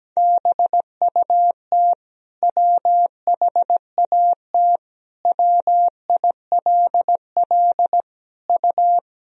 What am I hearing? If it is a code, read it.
Morse code, heard as a keyed tone, about 17 wpm, 700 Hz. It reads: BUT WHAT WILL U
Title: Sentences from Top 100 Words and Abbreviations 17wpm